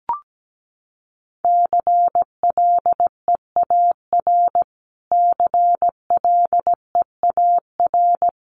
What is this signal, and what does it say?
Morse code, heard as a keyed tone, about 17 wpm, 700 Hz. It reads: CLEAR CLEAR
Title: Top 500 Words Repeated 17wpm